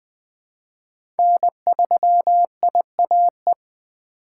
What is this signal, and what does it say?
Morse code, heard as a keyed tone, about 20 wpm, 700 Hz. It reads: N3IAE
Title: USA Call Signs - 2x Character Spacing - 20wpm